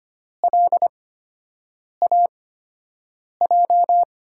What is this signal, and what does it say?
Morse code, heard as a keyed tone, about 25 wpm, spaced as for 7 wpm, 700 Hz. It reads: LAJ
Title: USA Call Signs - 6 Characters - 8x Character Spacing - 25wpm